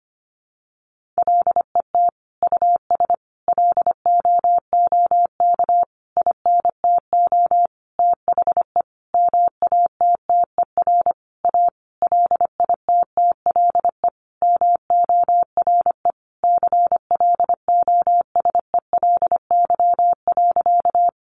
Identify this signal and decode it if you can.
Morse code, heard as a keyed tone, about 25 wpm, 700 Hz. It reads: LET US LOOK INTO THE MATTER A LITTLE MORE CLOSELY.
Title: Sentences from Top 1400 Words 25wpm